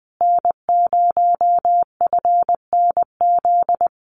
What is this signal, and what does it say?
Morse code, heard as a keyed tone, about 20 wpm, 700 Hz. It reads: N0FNZ
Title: Common QSO Phrases Encore 20wpm